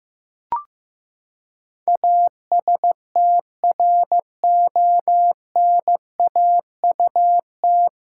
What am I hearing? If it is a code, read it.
Morse code, heard as a keyed tone, about 14 wpm, 700 Hz. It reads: ASTRONAUT
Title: Pangram Challenge v2 - 15wpm